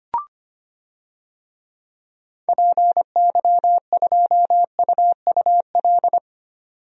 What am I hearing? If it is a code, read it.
Morse code, heard as a keyed tone, about 25 wpm, 700 Hz. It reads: PY2UUL
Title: DXCC-100 Call Signs - 6 Characters - 25wpm